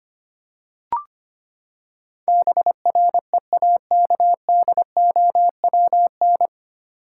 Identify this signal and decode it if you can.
Morse code, heard as a keyed tone, about 25 wpm, 700 Hz. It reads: BREAKDOWN
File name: Nine Letter Words - 25wpm